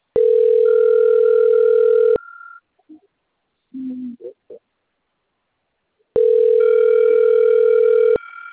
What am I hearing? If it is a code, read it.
Ringback tone (North America), 440 + 480 Hz.